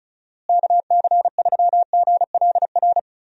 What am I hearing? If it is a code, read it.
Morse code, heard as a keyed tone, about 36 wpm, 700 Hz. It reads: KC3GLR
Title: USA Call Signs - 6 Characters - Repeated 3x - 35wpm